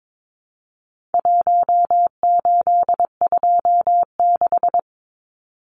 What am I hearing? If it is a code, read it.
Morse code, heard as a keyed tone, about 22 wpm, 700 Hz. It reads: 1826
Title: Numbers - Contest Style - 4-Digit 22wpm